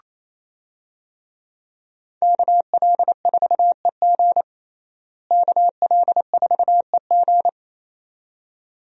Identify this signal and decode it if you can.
Morse code, heard as a keyed tone, about 27 wpm, 700 Hz. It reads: KL4EG KL4EG
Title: USA Call Signs - 5 Characters - Repeated 2x - 28wpm